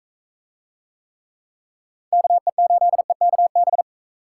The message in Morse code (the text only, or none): KE8EKB